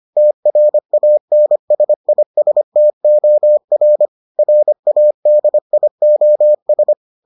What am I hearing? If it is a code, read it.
Morse code, heard as a keyed tone, about 25 wpm, 600 Hz. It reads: TRANSISTOR RADIOS